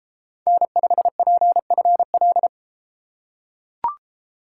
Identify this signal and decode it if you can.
Morse code, heard as a keyed tone, about 31 wpm, 700 Hz. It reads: N5PFL